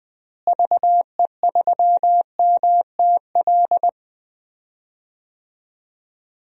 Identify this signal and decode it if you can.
Morse code, heard as a keyed tone, about 20 wpm, 700 Hz. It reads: VE3MTL
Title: DXCC-100 Call Signs - 6 Characters - Repeated 2x - 20wpm